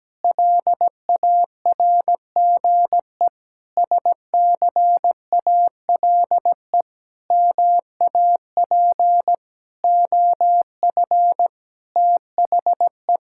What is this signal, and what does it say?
Morse code, heard as a keyed tone, about 17 wpm, 700 Hz. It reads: LARGE SCALE MAP OF THE
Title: Sentences from Top 1000 Words - Review Part 5 - 17wpm